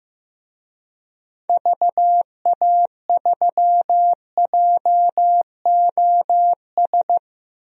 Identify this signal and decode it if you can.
Morse code, heard as a keyed tone, about 15 wpm, 700 Hz. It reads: VA3JOS
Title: DXCC-100 Call Signs - 6 Characters - 15wpm